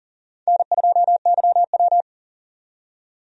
Morse code, heard as a keyed tone, about 37 wpm, 700 Hz. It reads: N2YW
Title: USA Call Signs - 4 Characters - 40wpm